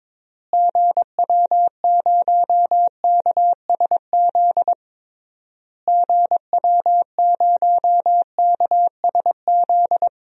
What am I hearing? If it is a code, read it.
Morse code, heard as a keyed tone, about 22 wpm, 700 Hz. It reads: GW0KSZ GW0KSZ
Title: DXCC-100 Call Signs - 6 Characters - Repeated 3x - 22wpm